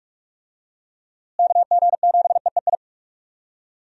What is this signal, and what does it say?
Morse code, heard as a keyed tone, about 43 wpm, 700 Hz. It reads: KG7EEI
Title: USA Call Signs - 6 Characters - Repeated 2x - 45wpm